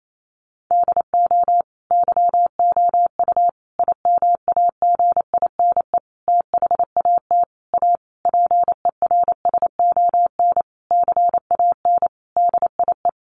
Morse code, heard as a keyed tone, about 28 wpm, 700 Hz. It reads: DO YOU IMAGINE THAT A PERSON CAN DIE
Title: Sentences from Top 1400 Words 28wpm